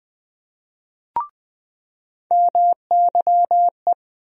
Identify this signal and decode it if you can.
Morse code, heard as a keyed tone, about 20 wpm, 700 Hz. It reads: MYE
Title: USA Call Sign Postfixes - 3 Letters 20wpm